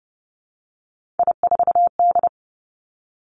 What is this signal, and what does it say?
Morse code, heard as a keyed tone, about 30 wpm, 700 Hz. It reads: I4D
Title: Three Challenging Letter-Number Combinations 30wpm